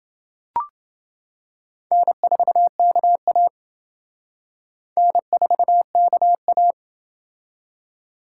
Morse code, heard as a keyed tone, about 29 wpm, 700 Hz. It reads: N4KA N4KA